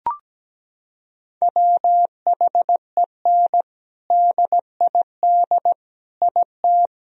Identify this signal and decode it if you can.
Morse code, heard as a keyed tone, about 17 wpm, 700 Hz. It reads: WHEN DID IT
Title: Sentences from Top 1000 Words - Review Part 1 - 17wpm